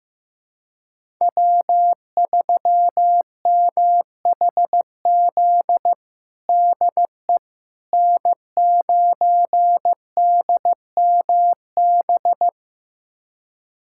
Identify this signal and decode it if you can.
Morse code, heard as a keyed tone, about 15 wpm, 700 Hz. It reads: W3MHZ DE N9DMB
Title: USA Callsign DE Callsign - 5 Characters - 15wpm